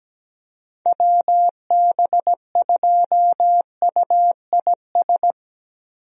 Morse code, heard as a keyed tone, about 17 wpm, 700 Hz. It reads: WB2UIS
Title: USA Call Signs - 6 Characters - Repeated 2x - 17wpm